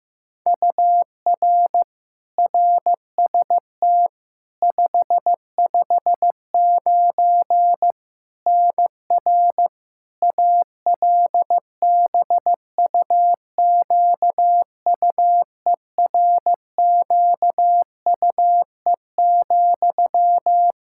Morse code, heard as a keyed tone, about 15 wpm, 700 Hz. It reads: UR RST 559 NR ALBUQUERQUE,
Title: Practice QSO - Exchange 1 - Beginning - 15wpm